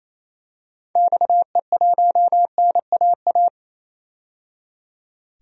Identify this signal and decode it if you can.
Morse code, heard as a keyed tone, about 28 wpm, 700 Hz. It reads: XE1NAA
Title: DXCC-100 Call Signs - 6 Characters - 28wpm